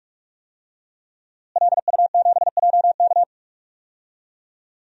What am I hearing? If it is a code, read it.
Morse code, heard as a keyed tone, about 46 wpm, 700 Hz. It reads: LU7JK
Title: DXCC-100 Call Signs - 5 Characters - Repeated 2x - 45wpm